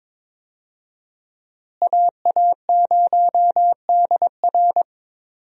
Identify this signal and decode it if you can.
Morse code, heard as a keyed tone, about 21 wpm, 700 Hz. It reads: AA0DR
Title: USA Call Signs - 5 Characters - Repeated 2x - 22wpm